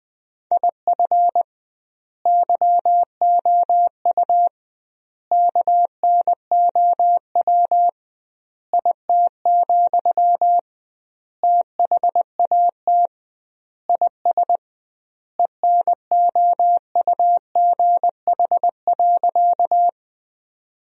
Morse code, heard as a keyed tone, about 20 wpm, 700 Hz. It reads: IF YOU KNOW IT, THAT IS ENOUGH.
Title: 2x Word-Spacing - Sentences from Top 400 Words - 20wpm